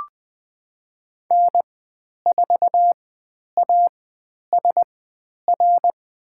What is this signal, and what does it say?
Morse code, heard as a keyed tone, about 20 wpm, spaced as for 10 wpm, 700 Hz. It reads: N4ASR
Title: 109 Lesson 18g - Letter m - Call Signs 20wpm